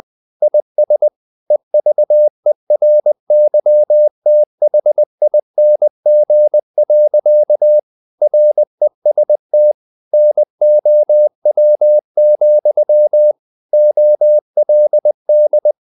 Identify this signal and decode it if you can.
Morse code, heard as a keyed tone, about 20 wpm, 600 Hz. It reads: IS EVERYTHING. REST NOW, OLD
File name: the old man and the sea part 2 20wpm